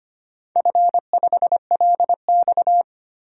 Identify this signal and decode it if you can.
Morse code, heard as a keyed tone, about 25 wpm, 700 Hz. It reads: F5LX
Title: DXCC-100 Call Signs - 4 Characters - 25wpm